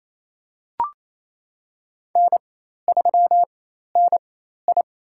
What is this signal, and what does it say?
Morse code, heard as a keyed tone, about 28 wpm, spaced as for 13 wpm, 700 Hz. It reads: N3NI
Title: USA Call Signs - 4x Character Spacing - 28wpm